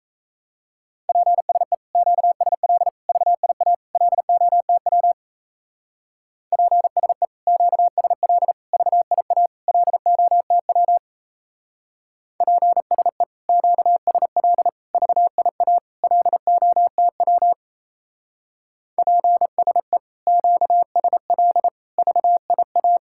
Morse code, heard as a keyed tone, about 36 wpm, 700 Hz. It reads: PSE QSL VIA LOTW PSE QSL VIA LOTW PSE QSL VIA LOTW PSE QSL VIA